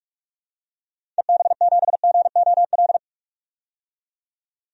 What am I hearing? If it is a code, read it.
Morse code, heard as a keyed tone, about 41 wpm, 700 Hz. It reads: EB7GOL
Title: DXCC-100 Call Signs - 6 Characters - 45wpm